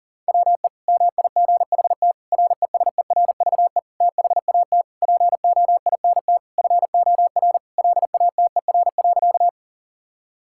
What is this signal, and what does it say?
Morse code, heard as a keyed tone, about 40 wpm, 700 Hz. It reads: WE MIGHT RESERVE THAT POINT FOR LATER.
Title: Sentences from Top 1000 Words - Review Part 6 - 40wpm